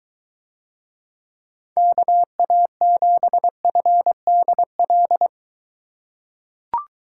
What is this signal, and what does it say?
Morse code, heard as a keyed tone, about 23 wpm, 700 Hz. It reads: KA7FDL